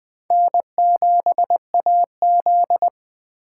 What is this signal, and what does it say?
Morse code, heard as a keyed tone, about 21 wpm, 700 Hz. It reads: N7AZ
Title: K1USN SST QSO Practice - 20wpm